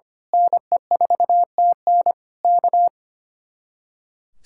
Morse code, heard as a keyed tone, about 25 wpm, 700 Hz. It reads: NE4TN K
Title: Speed-Racing Common QSO Phrases Encore - 25wpm